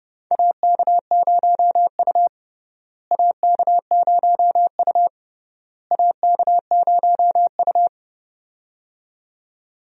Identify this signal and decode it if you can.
Morse code, heard as a keyed tone, about 30 wpm, 700 Hz. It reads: AK0U AK0U AK0U
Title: USA Call Signs - 4 Characters - Repeated 3x - 30wpm